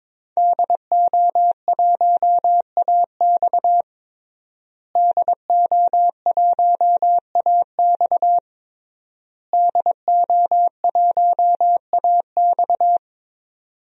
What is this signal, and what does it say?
Morse code, heard as a keyed tone, about 21 wpm, 700 Hz. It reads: DO1AX DO1AX DO1AX
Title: DXCC-100 Call Signs - 5 Characters - Repeated 3x - 22wpm